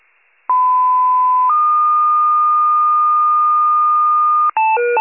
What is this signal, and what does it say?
Two-tone page, heard as a one-second tone, then a three-second tone, about 1010 then 1220 Hz. It is heard from 0.5 to 4.5 s.
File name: pager / 94 RESCUE OFC / wcalert